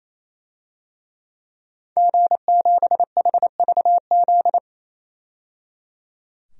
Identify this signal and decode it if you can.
Morse code, heard as a keyed tone, about 28 wpm, 700 Hz. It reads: G7HVZ
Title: DXCC-100 Call Signs - 5 Characters - 28wpm